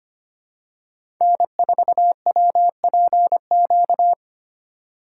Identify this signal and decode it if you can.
Morse code, heard as a keyed tone, about 25 wpm, 700 Hz. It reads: N4WPQ
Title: USA Call Signs - 5 Characters - 3x Character Spacing - 25wpm